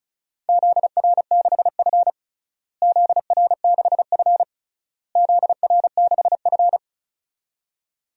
Morse code, heard as a keyed tone, about 36 wpm, 700 Hz. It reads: ZR6F ZR6F ZR6F
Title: DXCC-100 Call Signs - 4 Characters - Repeated 3x - 35wpm